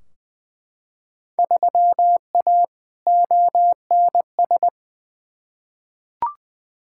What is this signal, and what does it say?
Morse code, heard as a keyed tone, about 20 wpm, 700 Hz. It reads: 3A ONS
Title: Field Day Exchanges - 20wpm